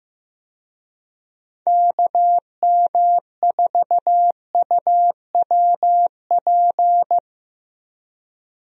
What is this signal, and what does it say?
Morse code, heard as a keyed tone, about 15 wpm, 700 Hz. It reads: KM4UWP
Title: USA Call Signs - 6 Characters - Repeated 3x - 15wpm